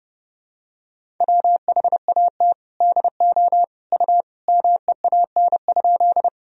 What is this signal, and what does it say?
Morse code, heard as a keyed tone, about 31 wpm, 700 Hz. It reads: WHAT DO U MEAN?
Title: Sentences from Top 200 Words and Abbreviations 30wpm